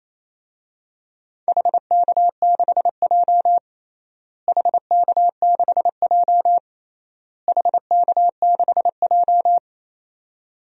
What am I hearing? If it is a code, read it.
Morse code, heard as a keyed tone, about 28 wpm, 700 Hz. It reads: HK6J HK6J HK6J
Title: DXCC-100 Call Signs - 4 Characters - Repeated 3x - 28wpm